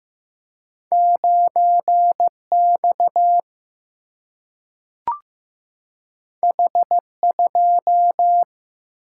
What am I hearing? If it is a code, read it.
Morse code, heard as a keyed tone, about 15 wpm, 700 Hz. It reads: 9X H2
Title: Single-Letter Single-Number 15wpm